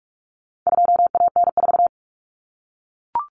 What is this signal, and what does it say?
Morse code, heard as a keyed tone, about 44 wpm, 700 Hz. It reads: 2AN4
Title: Common Cut Numbers - 4-Digit 45wpm